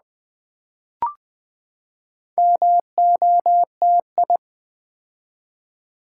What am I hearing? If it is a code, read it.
Morse code, heard as a keyed tone, about 20 wpm, 700 Hz. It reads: MOTI
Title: Build A Word - 4 Letters At A Time - 10-Letter Words - 20wpm